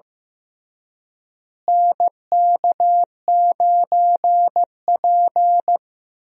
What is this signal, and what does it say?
Morse code, heard as a keyed tone, about 15 wpm, 700 Hz. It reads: NK9P
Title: USA Call Signs - 4 Characters - Repeated 2x - 15wpm